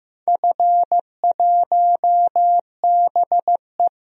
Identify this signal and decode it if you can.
Morse code, heard as a keyed tone, about 16 wpm, 700 Hz. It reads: F1BE
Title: DXCC-100 Call Signs - 4 Characters - 15wpm